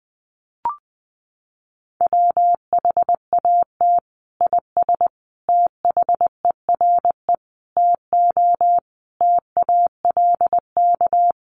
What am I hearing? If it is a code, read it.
Morse code, heard as a keyed tone, about 20 wpm, 700 Hz. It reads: WHAT IS THERE TO TALK
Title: Sentences from Top 400 Words 20wpm